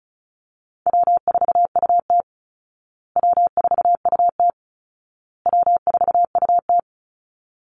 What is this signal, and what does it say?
Morse code, heard as a keyed tone, about 34 wpm, 700 Hz. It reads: W4UT W4UT W4UT
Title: Easy Call Signs Repeated 3x - 35wpm